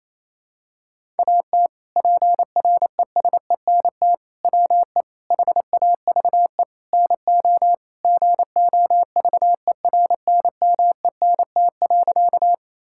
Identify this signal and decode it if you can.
Morse code, heard as a keyed tone, about 27 wpm, 700 Hz. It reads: AT PRESENT WE HAVE NO GOVERNMENT.
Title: Sentences from Top 1000 Words - Review Part 5 - 28wpm